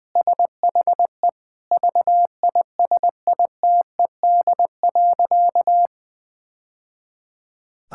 Morse code, heard as a keyed tone, about 21 wpm, 700 Hz. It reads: SHE VISITED.
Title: Sentences with 2-Word Phrases - 20wpm